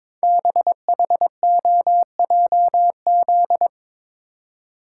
Morse code, heard as a keyed tone, about 22 wpm, 700 Hz. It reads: BHOJZ
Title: Code Groups - 5 Letters - 22wpm